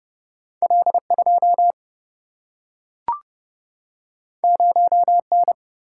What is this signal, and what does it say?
Morse code, heard as a keyed tone, about 30 wpm, 700 Hz. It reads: L2 0N
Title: Single-Letter Single-Number 30wpm